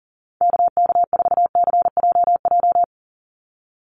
Morse code, heard as a keyed tone, about 40 wpm, 700 Hz. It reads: KK4CJJ
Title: USA Call Signs 40wpm